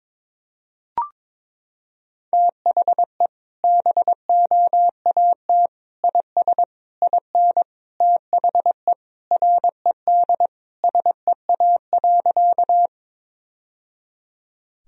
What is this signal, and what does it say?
Morse code, heard as a keyed tone, about 22 wpm, 700 Hz. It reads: THE BOAT IS IN THE RED SEA.
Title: Sentences from Top 1000 Words - Review Part 2 - 22wpm